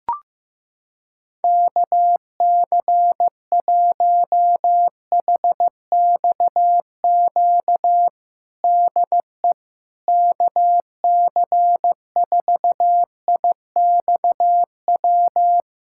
Morse code, heard as a keyed tone, about 15 wpm, 700 Hz. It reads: KC1HXQ DE KC4IXW
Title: USA Callsign DE Callsign - 6 Characters - 15wpm